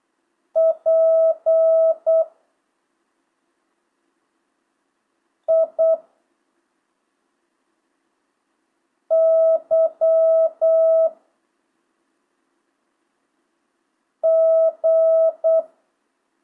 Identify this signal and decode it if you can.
Morse code, heard as a keyed tone, about 8 wpm, 650 Hz. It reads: P I Y G
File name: jh8cbh / 2025cw